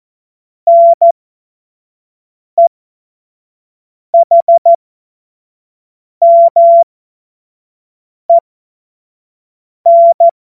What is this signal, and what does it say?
Morse code, heard as a keyed tone, about 14 wpm, spaced as for 5 wpm, 680 Hz. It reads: NEHMEN
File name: Morsekurs / Witze / Witz 17. August 2021, 25 F70